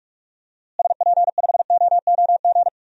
Morse code, heard as a keyed tone, about 41 wpm, 700 Hz. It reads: SP5OOG
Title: DXCC-100 Call Signs - 6 Characters - Repeated 3x - 45wpm